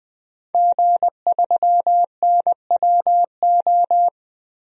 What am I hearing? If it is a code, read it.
Morse code, heard as a keyed tone, about 20 wpm, 700 Hz. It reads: G3NWO
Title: DXCC-100 Call Signs - 5 Characters - 20wpm